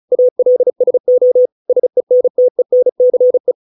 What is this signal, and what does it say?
Morse code, heard as a keyed tone, about 35 wpm, 500 Hz. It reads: ALSO SENTENCE